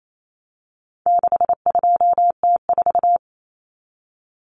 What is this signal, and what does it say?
Morse code, heard as a keyed tone, about 28 wpm, 700 Hz. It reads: 62T4
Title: Common Cut Numbers - 4-Digit 28wpm